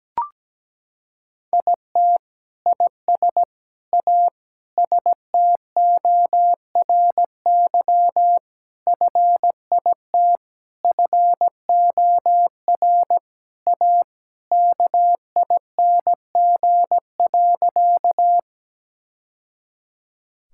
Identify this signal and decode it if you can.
Morse code, heard as a keyed tone, about 17 wpm, 700 Hz. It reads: IT IS A STORY FIT FOR A KING.
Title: Sentences from Top 1000 Words - Review Part 5 - 17wpm